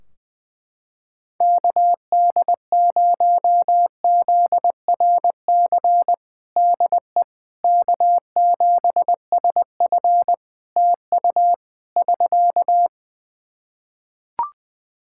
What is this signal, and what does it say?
Morse code, heard as a keyed tone, about 20 wpm, 700 Hz. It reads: KD0ZRC DE K7SF TU <SK>
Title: Practice QSO - Exchange 4 - Ending - 20wpm